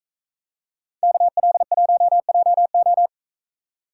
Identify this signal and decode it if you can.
Morse code, heard as a keyed tone, about 44 wpm, 700 Hz. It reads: KL1JO